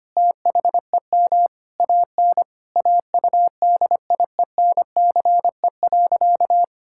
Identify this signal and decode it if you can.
Morse code, heard as a keyed tone, about 25 wpm, 700 Hz. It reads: THEM AN AUDIENCE.
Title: Sentences from Top 1000 Words - Review Part 5 - 25wpm